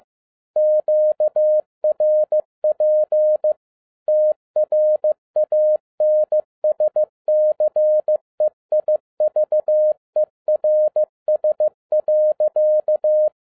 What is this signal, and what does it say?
Morse code, heard as a keyed tone, about 15 wpm, 610 Hz. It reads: QRP TRANSCEIVERS.
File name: 2011-07-mp3 / n22 QRP OTAKU 15wpm